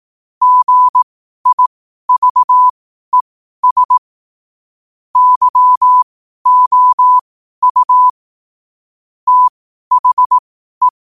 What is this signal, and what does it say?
Morse code, heard as a keyed tone, about 18 wpm, spaced as for 12 wpm, 1000 Hz. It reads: GIVES YOU THE
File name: qotdfiles / 12wpm